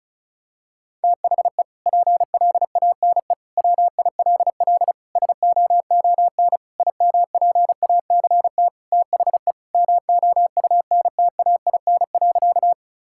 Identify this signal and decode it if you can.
Morse code, heard as a keyed tone, about 36 wpm, 700 Hz. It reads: THE PLANE WILL SOON IMPACT THE MOUNTAIN.
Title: Sentences from Top 1000 Words - Review Part 6 - 35wpm